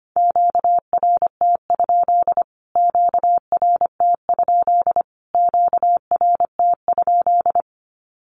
Morse code, heard as a keyed tone, about 25 wpm, 700 Hz. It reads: QRT? QRT? QRT?
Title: Q-Codes Repeated 3x 25wpm